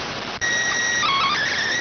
SSTV (slow-scan television), PD 290.